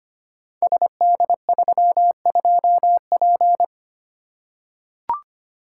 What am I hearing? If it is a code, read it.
Morse code, heard as a keyed tone, about 25 wpm, 700 Hz. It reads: SD32P
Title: Code Groups - 5 Letter:Numbers - 25wpm